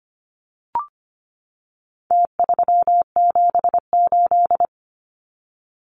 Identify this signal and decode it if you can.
Morse code, heard as a keyed tone, about 25 wpm, 700 Hz. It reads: T378
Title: Numbers - Contest Style - 4-Digit 25wpm